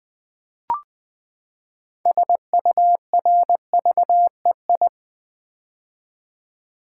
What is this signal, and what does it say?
Morse code, heard as a keyed tone, about 20 wpm, 700 Hz. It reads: SURVEI
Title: Build A Word - 6 Letters At A Time - 12-Letter Words - 20wpm